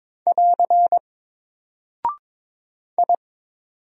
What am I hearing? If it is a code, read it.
Morse code, heard as a keyed tone, about 22 wpm, 700 Hz. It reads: <AR> I
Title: Single Character (Letter:Number:Punctuation:Prosign) Alt 22wpm